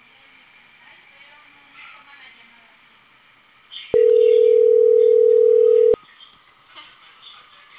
Ringback tone (North America), 440 + 480 Hz.